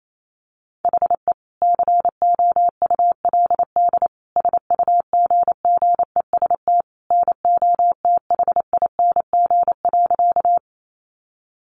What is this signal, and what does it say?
Morse code, heard as a keyed tone, about 28 wpm, 700 Hz. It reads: HE COULD SUGGEST NOTHING.
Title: Sentences from Top 1300 Words 28wpm